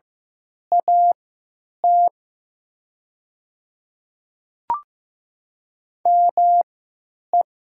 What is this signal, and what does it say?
Morse code, heard as a keyed tone, about 15 wpm, spaced as for 9 wpm, 700 Hz. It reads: AT ME
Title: 3x Character Spacing - Top 200 Words - 15wpm